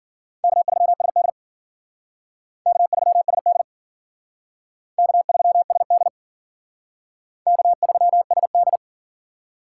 Morse code, heard as a keyed tone, about 50 wpm, 700 Hz. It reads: K3SD K3SD K3SD K3SD